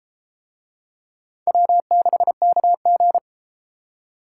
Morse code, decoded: W6KG

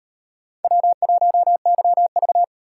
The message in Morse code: W1YV

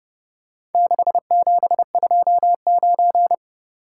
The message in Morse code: 6729

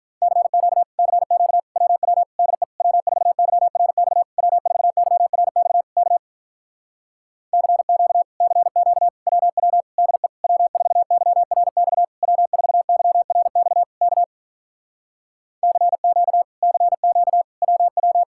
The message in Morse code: CQ CQ WW DE W4YRX W4YRX K CQ CQ WW DE W4YRX W4YRX K CQ CQ WW

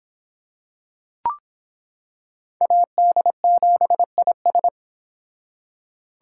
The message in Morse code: AD7IS

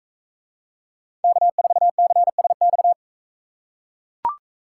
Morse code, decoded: K4CSX

